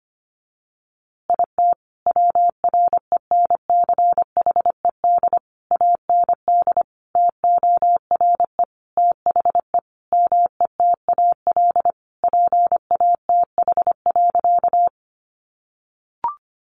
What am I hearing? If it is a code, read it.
Morse code, heard as a keyed tone, about 25 wpm, 700 Hz. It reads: IT WRENCHED AND TORE THE METAL PATH.
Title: A Sound Of Thunder 25wpm